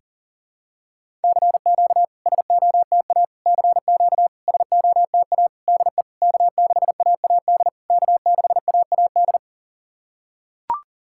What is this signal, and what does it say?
Morse code, heard as a keyed tone, about 43 wpm, 700 Hz. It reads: CQ SOTA CQ SOTA DE K6AAD K6AAD
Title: SOTA QSO Practice 40wpm